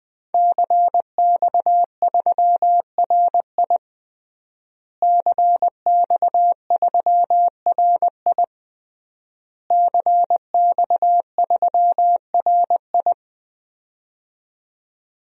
Morse code, heard as a keyed tone, about 21 wpm, 700 Hz. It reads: CX3RI CX3RI CX3RI
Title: DXCC-100 Call Signs - 5 Characters - Repeated 3x - 20wpm